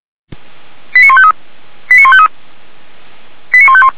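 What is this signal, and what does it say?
Five-tone selective call (ZVEI), heard as quick runs of five tones, tones of about 70 ms. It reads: 89153 89153 89151
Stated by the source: Five-tone ZVEI selective call received by SDR.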